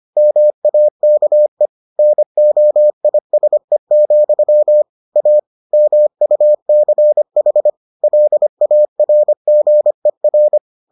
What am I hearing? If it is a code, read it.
Morse code, heard as a keyed tone, about 25 wpm, 600 Hz. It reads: MAKE NOISE, A MUCH LARGER